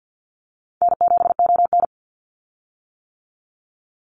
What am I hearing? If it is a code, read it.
Morse code, heard as a keyed tone, about 50 wpm, 700 Hz. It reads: N7ON
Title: Easy Call Signs - 50wpm